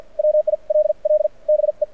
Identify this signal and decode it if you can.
Morse code, heard as a keyed tone, about 53 wpm, spaced as for 42 wpm, 590 Hz. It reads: WILL BE
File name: archived / CRSnet